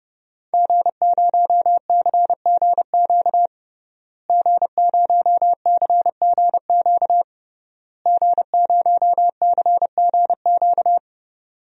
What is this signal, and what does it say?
Morse code, heard as a keyed tone, about 32 wpm, 700 Hz. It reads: G0CGQ G0CGQ G0CGQ
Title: DXCC-100 Call Signs - 5 Characters - Repeated 3x - 30wpm